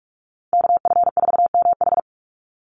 Morse code, heard as a keyed tone, about 46 wpm, 690 Hz. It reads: KF4MH
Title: NA SOTA Chaser Callsigns 45wpm